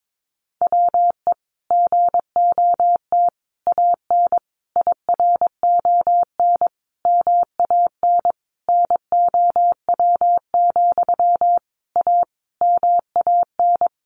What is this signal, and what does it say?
Morse code, heard as a keyed tone, about 22 wpm, 700 Hz. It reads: WE GOT AN IRON MAN NOW, A MAN
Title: A Sound Of Thunder 22wpm